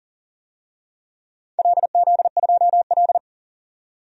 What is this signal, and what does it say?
Morse code, heard as a keyed tone, about 40 wpm, 700 Hz. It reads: LZ2L
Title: DXCC-100 Call Signs - 4 Characters - 40wpm